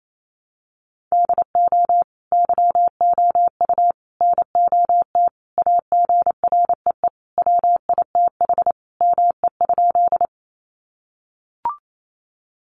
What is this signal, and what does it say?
Morse code, heard as a keyed tone, about 28 wpm, 700 Hz. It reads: DO YOU NOT AGREE WITH ME?
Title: Sentences from Top 1100 Words 28wpm